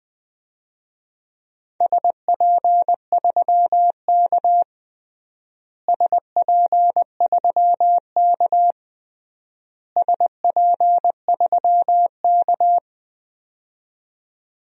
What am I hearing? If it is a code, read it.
Morse code, heard as a keyed tone, about 19 wpm, 700 Hz. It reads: SP3K SP3K SP3K
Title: DXCC-100 Call Signs - 4 Characters - Repeated 3x - 20wpm